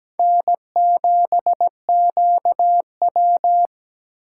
Morse code, heard as a keyed tone, about 20 wpm, 700 Hz. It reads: N7QW
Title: Speed Racing - USA 1x2 Call Signs 17wpm